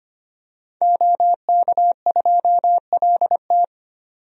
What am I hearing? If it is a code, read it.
Morse code, heard as a keyed tone, about 25 wpm, 700 Hz. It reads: OK2LT
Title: DXCC-100 Call Signs - 5 Characters - 25wpm